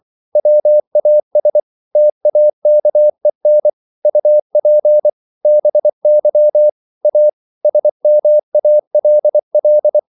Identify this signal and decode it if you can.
Morse code, heard as a keyed tone, about 24 wpm, 600 Hz. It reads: WAS TAKEN UP BY A SMALL